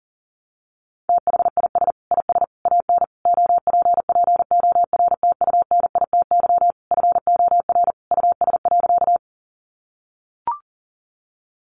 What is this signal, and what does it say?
Morse code, heard as a keyed tone, about 37 wpm, 700 Hz. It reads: THIS IS AN OPPORTUNITY FOR US.
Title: Sentences from Top 1000 Words 40wpm